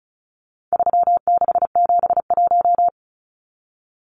Morse code, heard as a keyed tone, about 34 wpm, 700 Hz. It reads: 3671
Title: Numbers - 4-Digit 35wpm